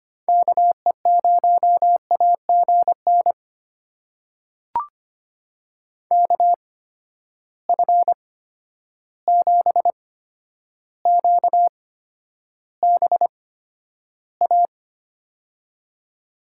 Morse code, heard as a keyed tone, about 25 wpm, spaced as for 11 wpm, 700 Hz. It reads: KE0AGN KF7QBA
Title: USA Call Signs - 8x Character Spacing - 25wpm